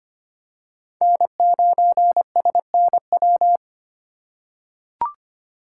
Morse code, heard as a keyed tone, about 25 wpm, 700 Hz. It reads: N9SNW
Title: USA Call Signs - 5 Characters - 2x Character Spacing - 25wpm